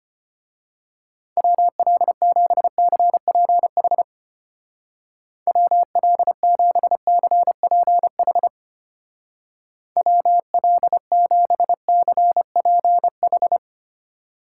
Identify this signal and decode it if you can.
Morse code, heard as a keyed tone, about 31 wpm, 700 Hz. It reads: WL7CPH WL7CPH WL7CPH
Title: Speed Racing - USA 2x3 Call Signs 25wpm